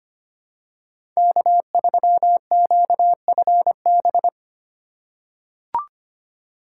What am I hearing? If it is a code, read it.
Morse code, heard as a keyed tone, about 25 wpm, 700 Hz. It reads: K3QFB
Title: USA Call Signs - 5 Characters - 3x Character Spacing - 25wpm